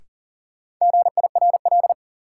Morse code, heard as a keyed tone, about 37 wpm, 700 Hz. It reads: GIRL
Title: Sentences from Top 400 Words - New Word Drill - 40wpm